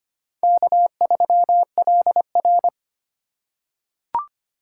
Morse code, heard as a keyed tone, about 25 wpm, 700 Hz. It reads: K3LR
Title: K1USN SST QSO Practice - 25wpm